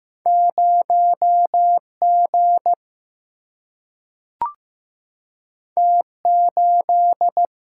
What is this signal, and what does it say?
Morse code, heard as a keyed tone, about 15 wpm, 700 Hz. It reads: 0 G T 8
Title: Single-Letter Single-Number 15wpm